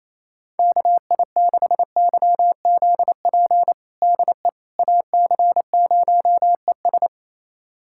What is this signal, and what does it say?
Morse code, heard as a keyed tone, about 28 wpm, 700 Hz. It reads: KI6YZP DE AC0ES
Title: USA Callsign DE Callsign 28wpm